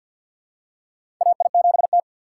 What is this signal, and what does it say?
Morse code, heard as a keyed tone, about 51 wpm, 690 Hz. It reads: AI7T
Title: USA Call Signs - 2x Character Spacing - 50wpm